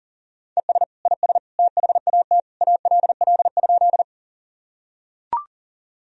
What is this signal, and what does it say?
Morse code, heard as a keyed tone, about 40 wpm, 700 Hz. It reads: ES IS THAT ALL?
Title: Sentences from Top 100 Words and Abbreviations 40wpm